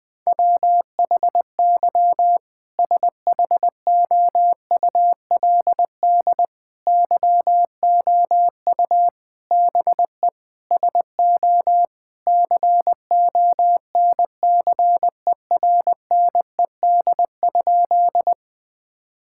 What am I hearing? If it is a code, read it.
Morse code, heard as a keyed tone, about 21 wpm, 700 Hz. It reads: WHY SHOULD YOU BE SO CONCERNED?
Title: Sentences from Top 1000 Words - Review Part 6 - 20wpm